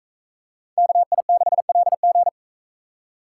Morse code, heard as a keyed tone, about 43 wpm, 700 Hz. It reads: KI6LG